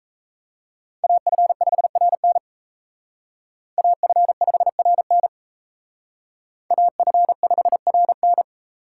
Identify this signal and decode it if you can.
Morse code, heard as a keyed tone, about 35 wpm, 700 Hz. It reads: AF5RN AF5RN AF5RN